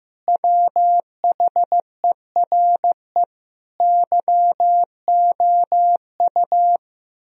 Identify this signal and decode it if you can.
Morse code, heard as a keyed tone, about 15 wpm, 700 Hz. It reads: WHERE YOU
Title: Sentences from Top 1000 Words - Review Part 4 - 15wpm